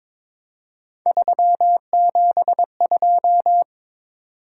Morse code, heard as a keyed tone, about 22 wpm, 700 Hz. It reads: 372
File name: practice / 22wpm / Three Letter-Number Combinations - 22wpm